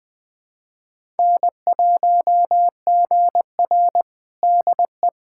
Morse code, heard as a keyed tone, about 21 wpm, 700 Hz. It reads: N1GR DE
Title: USA Callsign DE Callsign - 4 Characters - 20wpm